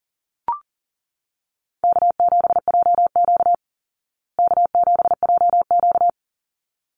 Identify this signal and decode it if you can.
Morse code, heard as a keyed tone, about 40 wpm, 700 Hz. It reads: K7JQ K7JQ
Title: Call Signs Repeated 3x - 40wpm